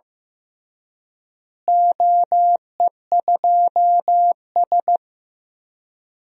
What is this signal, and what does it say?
Morse code, heard as a keyed tone, about 15 wpm, 700 Hz. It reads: OE2S
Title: DXCC-100 Call Signs - 4 Characters - Repeated 3x - 15wpm